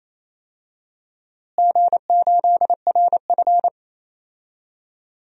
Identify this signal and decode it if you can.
Morse code, heard as a keyed tone, about 28 wpm, 700 Hz. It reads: G8RF
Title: DXCC-100 Call Signs - 4 Characters - Repeated 3x - 28wpm